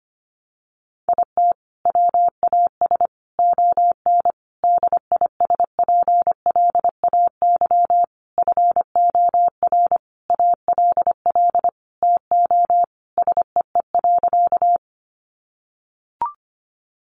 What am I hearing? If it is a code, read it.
Morse code, heard as a keyed tone, about 25 wpm, 700 Hz. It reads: IT WAS ON DISPLAY FOR ALL TO SEE.
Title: Sentences from Top 1200 Words 25wpm